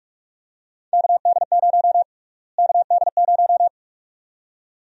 Morse code, heard as a keyed tone, about 46 wpm, 700 Hz. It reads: KD0 KD0
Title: USA Call Sign Prefixes - Repeated 2x - 45wpm